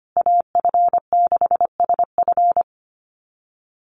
Morse code, heard as a keyed tone, about 25 wpm, 700 Hz. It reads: AF6SF
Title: USA Call Signs Repeated 3x 25wpm